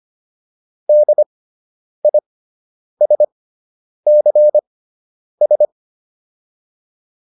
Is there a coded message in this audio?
Yes: Morse code from 0.9 to 5.7 s, heard as a keyed tone, about 25 wpm, spaced as for 9 wpm, 600 Hz. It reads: DISCS